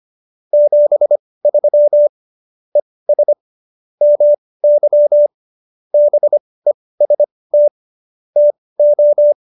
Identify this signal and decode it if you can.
Morse code, heard as a keyed tone, about 25 wpm, spaced as for 18 wpm, 590 Hz. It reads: 73 ES MY BEST TO